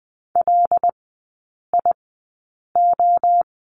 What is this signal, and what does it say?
Morse code, heard as a keyed tone, about 20 wpm, spaced as for 8 wpm, 700 Hz. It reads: LIO
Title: 198 Lesson 32e - Letter k - Three Characters 20wpm